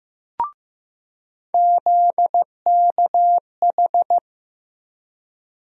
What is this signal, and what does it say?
Morse code, heard as a keyed tone, about 15 wpm, 700 Hz. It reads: ZKH
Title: USA Call Sign Postfixes - 3 Letters 15wpm